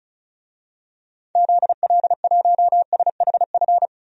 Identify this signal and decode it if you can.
Morse code, heard as a keyed tone, about 36 wpm, 690 Hz. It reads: ZL1SHF
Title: DXCC-100 Call Signs - 6 Characters - 35wpm